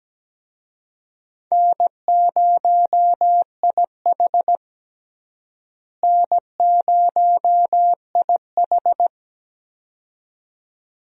Morse code, heard as a keyed tone, about 17 wpm, 700 Hz. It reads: N0IH N0IH
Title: USA Call Signs - 4 Characters - Repeated 2x - 17wpm